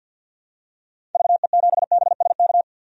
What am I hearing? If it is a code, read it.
Morse code, heard as a keyed tone, about 48 wpm, 700 Hz. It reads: VE7BSK
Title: DXCC-100 Call Signs - 6 Characters - 50wpm